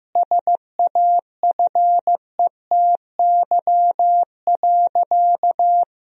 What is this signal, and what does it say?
Morse code, heard as a keyed tone, about 16 wpm, 700 Hz. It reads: SAFETY.
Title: Sentences from Top 1000 Words - Review Part 4 - 15wpm